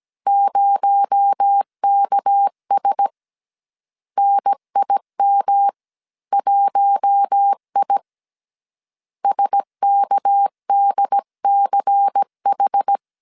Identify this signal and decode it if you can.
Morse code, heard as a keyed tone, about 17 wpm, 800 Hz. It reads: 0KS NIM1I SKDCH
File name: CW / DET / 1 6 1500C S A 16wpm 2635